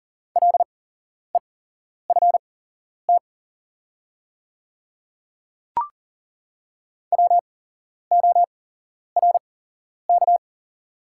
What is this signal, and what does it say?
Morse code, heard as a keyed tone, about 40 wpm, spaced as for 11 wpm, 700 Hz. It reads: LEFT WORK
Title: 8x Character Spacing - Top 500 Words - 40wpm